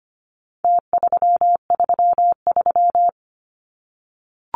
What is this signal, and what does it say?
Morse code, heard as a keyed tone, about 25 wpm, 700 Hz. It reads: T333
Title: Numbers - Contest Style - 4-Digit 25wpm